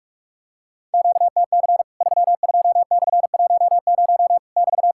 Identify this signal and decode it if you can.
Morse code, heard as a keyed tone, about 46 wpm, 700 Hz. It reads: QTC 32/10 <BT>
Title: Worked All Europe DX Contest - QTC Exchange - 45wpm